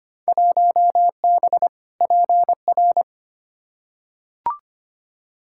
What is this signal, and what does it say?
Morse code, heard as a keyed tone, about 25 wpm, 700 Hz. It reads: 1B PR
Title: Field Day Exchanges Encore - 25wpm